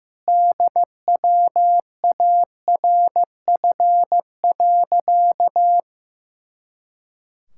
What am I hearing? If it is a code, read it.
Morse code, heard as a keyed tone, about 15 wpm, 700 Hz. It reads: DWARF.
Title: Pangram Challenge v2 - 15wpm